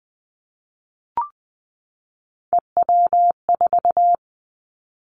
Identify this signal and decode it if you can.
Morse code, heard as a keyed tone, about 20 wpm, 700 Hz. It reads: EW4
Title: Callsign Prefixes for Top 50 DXCC - 20wpm